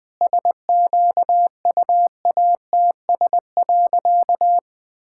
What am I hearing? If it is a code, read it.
Morse code, heard as a keyed tone, about 20 wpm, 700 Hz. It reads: SQUATS.
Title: All-Age Appropriate Jokes - 20wpm